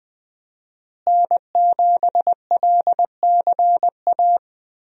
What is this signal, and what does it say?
Morse code, heard as a keyed tone, about 20 wpm, 700 Hz. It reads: N7LCA
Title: USA Call Signs - 5 Characters - Repeated 2x - 20wpm